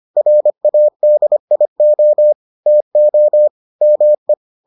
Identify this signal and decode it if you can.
Morse code, heard as a keyed tone, about 25 wpm, 600 Hz. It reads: RADIO TO ME